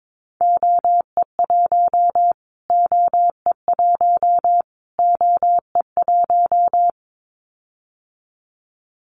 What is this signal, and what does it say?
Morse code, heard as a keyed tone, about 22 wpm, 700 Hz. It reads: OE1 OE1 OE1
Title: Callsign Prefixes for Top 25 DXCC - Repeated 3x - 22wpm